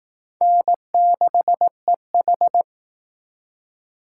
Morse code, heard as a keyed tone, about 19 wpm, 700 Hz. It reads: N6EH
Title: Speed Racing - USA 1x2 Call Signs 15wpm